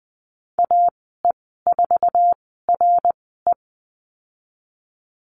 Morse code, heard as a keyed tone, about 20 wpm, spaced as for 14 wpm, 700 Hz. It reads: AE4RE
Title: 055 Lesson 10g - Letter r - Call Signs 20wpm